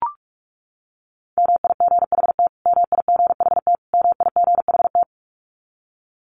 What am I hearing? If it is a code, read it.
Morse code, heard as a keyed tone, about 44 wpm, 700 Hz. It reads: MIGHT MIGHT MIGHT
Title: Top 500 Words Repeated 45wpm